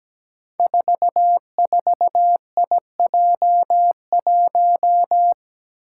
Morse code, heard as a keyed tone, about 17 wpm, 700 Hz. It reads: 44IJ1
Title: Five Challenging Letter-Number Combinations 17wpm